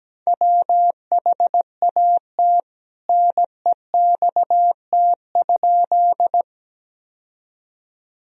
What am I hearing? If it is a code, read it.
Morse code, heard as a keyed tone, about 17 wpm, 700 Hz. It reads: WHAT NEXT?
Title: Sentences from Top 1000 Words - Review Part 1 - 17wpm